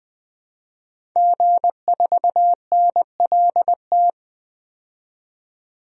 Morse code, heard as a keyed tone, about 20 wpm, 700 Hz. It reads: G4NLT
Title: DXCC-100 Call Signs - 5 Characters - 20wpm